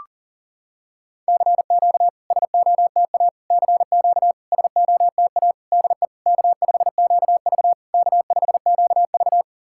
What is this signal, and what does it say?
Morse code, heard as a keyed tone, about 43 wpm, 690 Hz. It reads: CQ SOTA CQ SOTA DE K5QV K5QV
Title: SOTA QSO Practice 40wpm